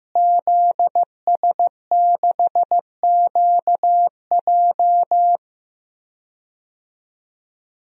Morse code, heard as a keyed tone, about 15 wpm, 700 Hz. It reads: ZS6QJ
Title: DXCC-100 Call Signs - 5 Characters - Repeated 2x - 15wpm